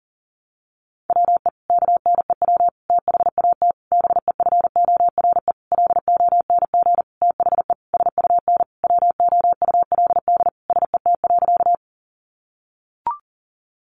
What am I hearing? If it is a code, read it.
Morse code, heard as a keyed tone, about 40 wpm, 700 Hz. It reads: WE KNEW THAT BEFORE LONG THE SUN WOULD SET.
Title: Sentences from Top 400 Words 40wpm